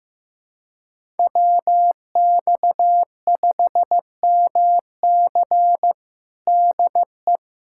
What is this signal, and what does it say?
Morse code, heard as a keyed tone, about 15 wpm, 700 Hz. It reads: WX5MC DE
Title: USA Callsign DE Callsign - 5 Characters - 15wpm